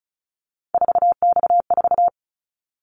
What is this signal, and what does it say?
Morse code, heard as a keyed tone, about 34 wpm, 700 Hz. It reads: 4X4
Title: Callsign Prefixes for Top 50 DXCC - 35wpm